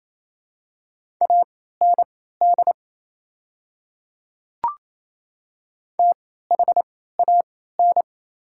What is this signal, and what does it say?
Morse code, heard as a keyed tone, about 27 wpm, spaced as for 16 wpm, 700 Hz. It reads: AND THAN
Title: 3x Character Spacing - Top 200 Words - 28wpm